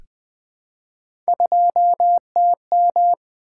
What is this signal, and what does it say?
Morse code, heard as a keyed tone, about 20 wpm, 700 Hz. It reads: 2TM
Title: Three Letter-Number Combinations - 20wpm